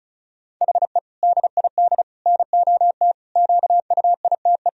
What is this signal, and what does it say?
Morse code, heard as a keyed tone, about 35 wpm, 700 Hz. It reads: HE DID NOT QUITE
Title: Sentences from Top 1000 Words - Review Part 1 - 35wpm